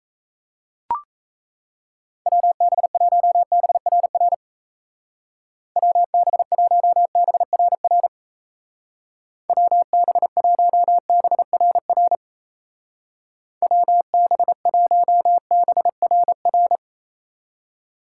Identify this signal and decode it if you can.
Morse code, heard as a keyed tone, about 35 wpm, 700 Hz. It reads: WB1BRR WB1BRR WB1BRR WB1BRR
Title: Speed Racing - USA 2x3 Call Signs 28wpm